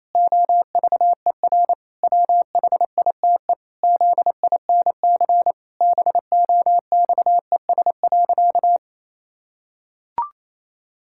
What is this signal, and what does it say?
Morse code, heard as a keyed tone, about 28 wpm, 700 Hz. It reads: OVER WHITE ZINC BOXES.
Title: Pangram Challenge v2 - 28wpm